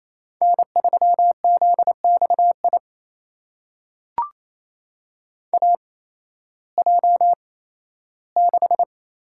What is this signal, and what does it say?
Morse code, heard as a keyed tone, about 28 wpm, spaced as for 15 wpm, 700 Hz. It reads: N3ZXI AJ6
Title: USA Call Signs - 5 Characters - 8x Character Spacing - 28wpm